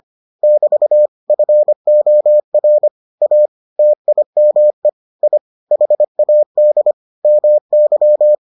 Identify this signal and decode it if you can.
Morse code, heard as a keyed tone, about 25 wpm, 600 Hz. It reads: <BT> FOR A TIME I HAD MY